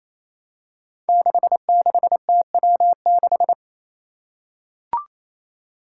Morse code, heard as a keyed tone, about 28 wpm, 700 Hz. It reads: 66TW6
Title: Five Challenging Letter-Number Combinations 28wpm